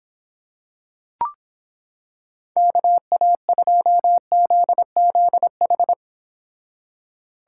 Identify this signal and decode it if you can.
Morse code, heard as a keyed tone, about 26 wpm, 700 Hz. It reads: KA2ZZH